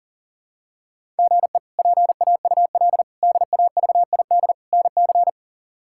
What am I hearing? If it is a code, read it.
Morse code, heard as a keyed tone, about 40 wpm, 690 Hz. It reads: GE PAUL DAVID NC
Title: K1USN SST QSO Practice - 40wpm